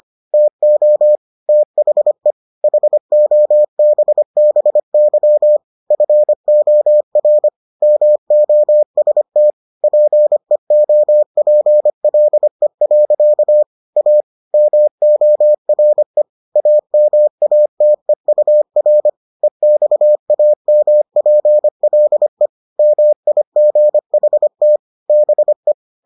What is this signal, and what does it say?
Morse code, heard as a keyed tone, about 25 wpm, 600 Hz. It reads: TO THE HOBBY FOR MOST PEOPLE. A MORE AMATEUR EXAMPLE MIGHT BE